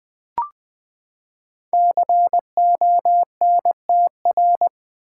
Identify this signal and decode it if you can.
Morse code, heard as a keyed tone, about 20 wpm, 700 Hz. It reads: CONTR
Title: Build A Word - 5 Letters At A Time - 10-Letter Words - 20wpm